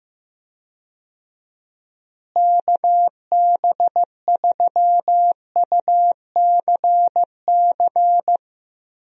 Morse code, heard as a keyed tone, about 16 wpm, 700 Hz. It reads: KB3UCC
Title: USA Call Signs - 6 Characters - Repeated 3x - 15wpm